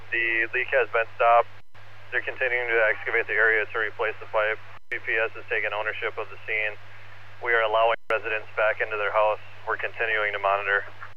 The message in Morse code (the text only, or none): none